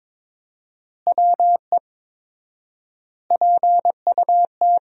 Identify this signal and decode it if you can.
Morse code, heard as a keyed tone, about 22 wpm, 700 Hz. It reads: WE PUT
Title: 4x Word-Spacing - Sentences from Top 200 Words - 22wpm